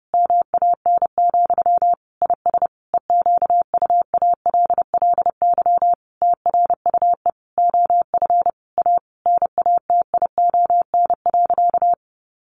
Morse code, heard as a keyed tone, about 30 wpm, 700 Hz. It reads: MAN, IS EQUALLY TRUE OF A NATION.
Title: Sentences from Top 1400 Words 30wpm